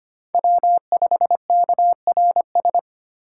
Morse code, heard as a keyed tone, about 25 wpm, 700 Hz. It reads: W5KRS
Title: USA Call Signs - 5 Characters - 3x Character Spacing - 25wpm